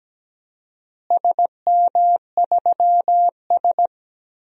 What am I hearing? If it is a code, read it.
Morse code, heard as a keyed tone, about 17 wpm, 700 Hz. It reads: SM3S
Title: DXCC-100 Call Signs - 4 Characters - Repeated 3x - 17wpm